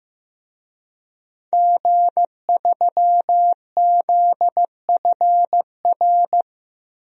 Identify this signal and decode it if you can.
Morse code, heard as a keyed tone, about 15 wpm, 700 Hz. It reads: G3ZFR
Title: DXCC-100 Call Signs - 5 Characters - 15wpm